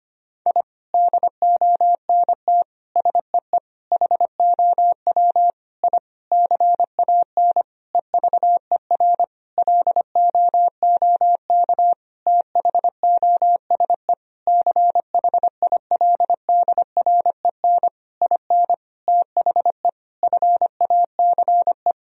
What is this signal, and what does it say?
Morse code, heard as a keyed tone, about 25 wpm, 700 Hz. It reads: I DONT SEE HOW I CAN EVER LOOK THOSE CHILDREN IN THE FACE
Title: Sentences from Top 1000 Words - Review Part 2 - 25wpm